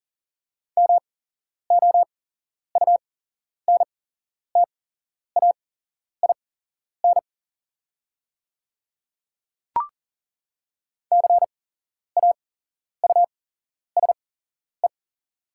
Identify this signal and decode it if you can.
Morse code, heard as a keyed tone, about 38 wpm, spaced as for 11 wpm, 700 Hz. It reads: MOUNTAIN CAUSE
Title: 8x Character Spacing - Top 500 Words - 40wpm